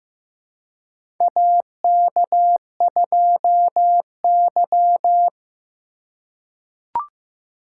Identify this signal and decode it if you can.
Morse code, heard as a keyed tone, about 15 wpm, 700 Hz. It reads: AK2Y
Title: USA Call Signs - 4 Characters - 4x Character Spacing - 15wpm